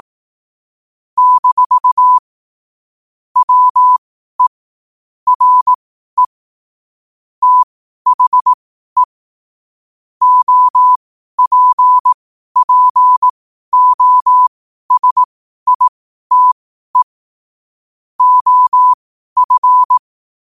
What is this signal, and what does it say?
Morse code, heard as a keyed tone, about 18 wpm, spaced as for 12 wpm, 1000 Hz. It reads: - WE RE THE OPPOSITE OF
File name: qotdfiles / 12wpm